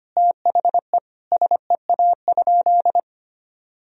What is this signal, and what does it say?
Morse code, heard as a keyed tone, about 25 wpm, 700 Hz. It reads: THE SEA?
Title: Sentences from Top 1000 Words - Review Part 1 - 25wpm